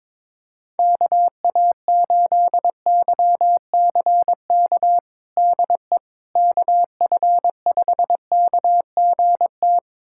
Morse code, heard as a keyed tone, about 22 wpm, 700 Hz. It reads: KA8YCK DE KF5KGT
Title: USA Callsign DE Callsign - 6 Characters - 22wpm